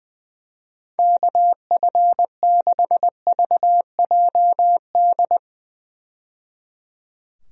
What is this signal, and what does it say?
Morse code, heard as a keyed tone, about 21 wpm, 700 Hz. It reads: KF6VJD
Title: USA Call Signs - 6 Characters - 20wpm